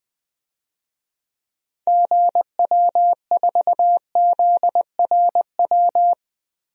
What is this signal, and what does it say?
Morse code, heard as a keyed tone, about 20 wpm, 700 Hz. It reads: GW4ZRW
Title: DXCC-100 Call Signs - 6 Characters - 20wpm